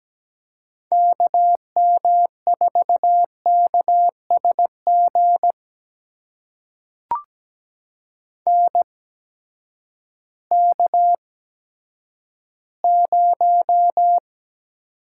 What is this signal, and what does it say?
Morse code, heard as a keyed tone, about 17 wpm, spaced as for 10 wpm, 700 Hz. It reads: KM4KSG NK0
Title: USA Call Signs - 8x Character Spacing - 17wpm